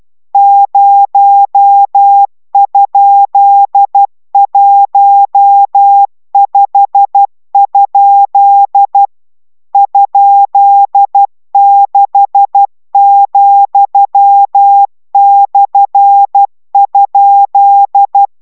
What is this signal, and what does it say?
Morse code, heard as a keyed tone, about 12 wpm, 800 Hz. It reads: 0?15? ?6,/?